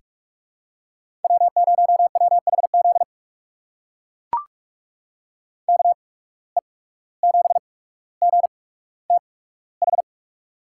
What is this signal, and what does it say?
Morse code, heard as a keyed tone, about 42 wpm, spaced as for 18 wpm, 700 Hz. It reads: W0WHZ KE7GTH
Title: USA Call Signs - 8x Character Spacing - 45wpm